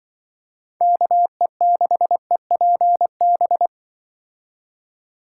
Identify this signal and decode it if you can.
Morse code, heard as a keyed tone, about 24 wpm, 700 Hz. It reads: KE6EPB
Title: Speed Racing - USA 2x3 Call Signs 20wpm